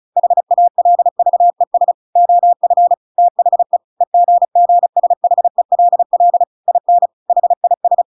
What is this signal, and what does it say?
Morse code, heard as a keyed tone, about 34 wpm, 700 Hz. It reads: HALVES OF THE EGGSHELL IN HIS